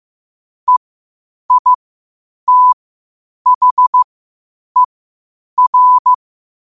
Morse code, heard as a keyed tone, about 15 wpm, spaced as for 8 wpm, 1000 Hz. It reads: EITHER
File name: qotdfiles / 8wpm